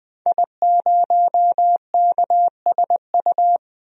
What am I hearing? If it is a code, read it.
Morse code, heard as a keyed tone, about 20 wpm, 700 Hz. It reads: I0KSU
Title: DXCC-100 Call Signs - 5 Characters - 20wpm